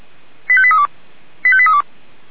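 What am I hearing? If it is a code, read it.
Five-tone selective call (ZVEI), heard as quick runs of five tones, tones of about 70 ms. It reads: 85832 85832
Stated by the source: Five-tone ZVEI selective call, code 89485, received by SDR, Feuerwehr Orlamünde.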